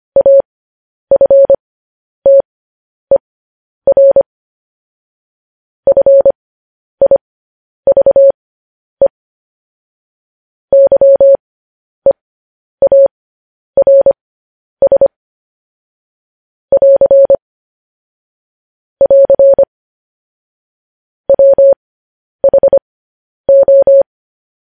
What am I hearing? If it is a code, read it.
Morse code, heard as a keyed tone, about 25 wpm, spaced as for 10 wpm, 550 Hz. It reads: AFTER FIVE YEARS <AR> <AR> WHO
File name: podcast / News headlines-1x-25x10